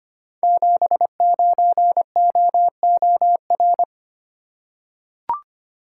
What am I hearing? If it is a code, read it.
Morse code, heard as a keyed tone, about 25 wpm, 700 Hz. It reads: 79OOR
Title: Code Groups - 5 Letter:Numbers - 25wpm